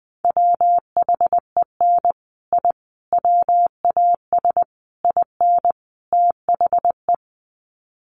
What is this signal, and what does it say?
Morse code, heard as a keyed tone, about 20 wpm, 700 Hz. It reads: WHEN I WAS IN THE
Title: Sets of 5 Words Encore2 20wpm